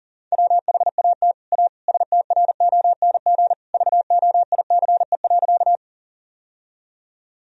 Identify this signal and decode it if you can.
Morse code, heard as a keyed tone, about 40 wpm, 700 Hz. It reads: WHAT A STRONG VOICE.
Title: Sentences from Top 1000 Words - Review Part 3 - 40wpm